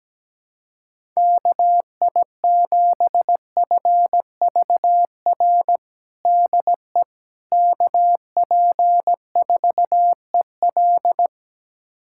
Morse code, heard as a keyed tone, about 17 wpm, 700 Hz. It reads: KI7FVR DE KP4EL
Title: USA Callsign DE Callsign 17wpm